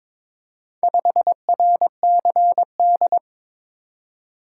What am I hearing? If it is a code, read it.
Morse code, heard as a keyed tone, about 23 wpm, 700 Hz. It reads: 5RCD
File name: practice / 22wpm / Four Letter-Number Combinations - 22wpm